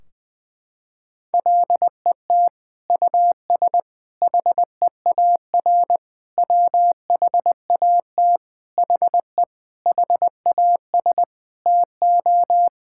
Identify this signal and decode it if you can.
Morse code, heard as a keyed tone, about 19 wpm, 700 Hz. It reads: LET US HEAR WHAT HE HAS TO
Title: Sentences from Top 1600 Words 20wpm